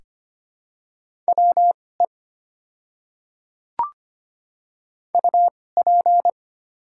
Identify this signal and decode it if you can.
Morse code, heard as a keyed tone, about 25 wpm, spaced as for 18 wpm, 700 Hz. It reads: WE UP
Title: 2x Character Spacing - Top 100 Words - 25wpm